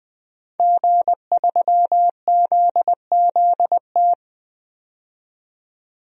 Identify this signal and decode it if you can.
Morse code, heard as a keyed tone, about 20 wpm, 700 Hz. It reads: G3ZZT
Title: DXCC-100 Call Signs - 5 Characters - 20wpm